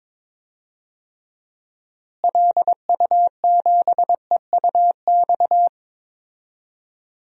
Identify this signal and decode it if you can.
Morse code, heard as a keyed tone, about 22 wpm, 700 Hz. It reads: LU7EUX
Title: DXCC-100 Call Signs - 6 Characters - Repeated 3x - 22wpm